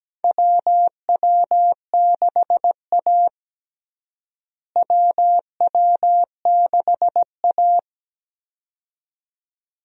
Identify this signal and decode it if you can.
Morse code, heard as a keyed tone, about 17 wpm, 700 Hz. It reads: WW6A WW6A
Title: USA Call Signs - 4 Characters - Repeated 2x - 17wpm